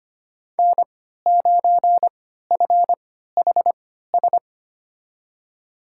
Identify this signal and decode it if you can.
Morse code, heard as a keyed tone, about 25 wpm, spaced as for 14 wpm, 700 Hz. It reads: N9FHS
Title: USA Call Signs - 5 Characters - 3x Character Spacing - 25wpm